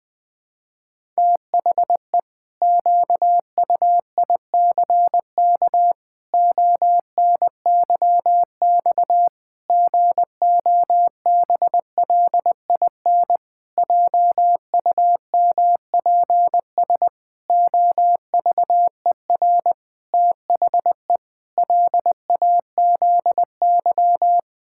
Morse code, heard as a keyed tone, about 21 wpm, 700 Hz. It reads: THE QUICK ONYX GOBLIN JUMPS OVER THE LAZY
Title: Pangram Challenge v2 - 20wpm